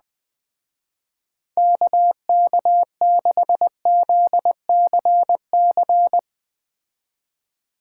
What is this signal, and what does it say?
Morse code, heard as a keyed tone, about 20 wpm, 700 Hz. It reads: KK6ZCC